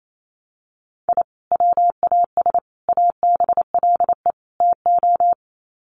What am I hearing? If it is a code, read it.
Morse code, heard as a keyed tone, about 28 wpm, 700 Hz. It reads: I WAS ABLE TO
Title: Sets of 4 Words 28wpm